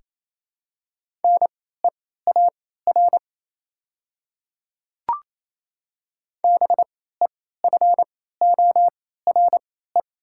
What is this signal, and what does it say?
Morse code, heard as a keyed tone, about 28 wpm, spaced as for 16 wpm, 690 Hz. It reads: NEAR BEFORE
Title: 3x Character Spacing - Top 300 Words - 28wpm